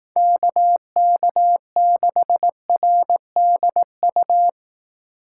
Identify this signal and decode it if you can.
Morse code, heard as a keyed tone, about 18 wpm, 700 Hz. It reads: KK6RDU
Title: Speed Racing - USA 2x3 Call Signs 15wpm